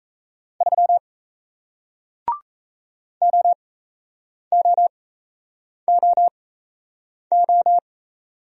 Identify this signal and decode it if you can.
Morse code, heard as a keyed tone, about 38 wpm, spaced as for 8 wpm, 700 Hz. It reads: 3 OOOO